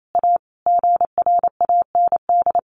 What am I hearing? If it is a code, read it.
Morse code, heard as a keyed tone, about 28 wpm, 700 Hz. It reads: A GRAND
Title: A Sound Of Thunder 28wpm